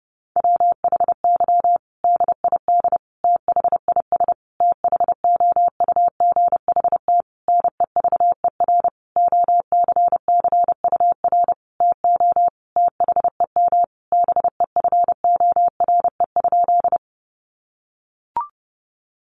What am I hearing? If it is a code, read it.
Morse code, heard as a keyed tone, about 30 wpm, 700 Hz. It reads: WHY DID THIS THOUGHT NEVER OCCUR TO THEM BEFORE?
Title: Sentences from Top 900 Words 30wpm